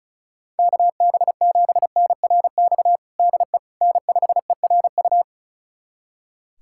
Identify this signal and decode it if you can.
Morse code, heard as a keyed tone, about 36 wpm, 690 Hz. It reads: KB7NRX DE N5ERU
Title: USA Callsign DE Callsign 35wpm